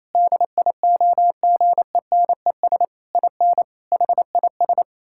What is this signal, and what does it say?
Morse code, heard as a keyed tone, about 28 wpm, 700 Hz. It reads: DIOGENES IN HIS
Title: Pangram Challenge v2 - 28wpm